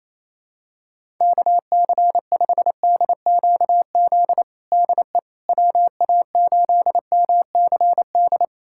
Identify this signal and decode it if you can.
Morse code, heard as a keyed tone, about 28 wpm, 700 Hz. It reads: KC5DQZ DE WA8MCD
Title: USA Callsign DE Callsign 28wpm